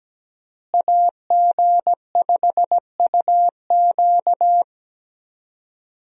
Morse code, heard as a keyed tone, about 17 wpm, 700 Hz. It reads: AG5UQ
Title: USA Call Signs - 5 Characters - Repeated 3x - 17wpm